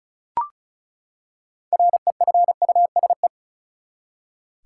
Morse code, heard as a keyed tone, about 36 wpm, 700 Hz. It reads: REFUSE
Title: Six Letter Words - 35wpm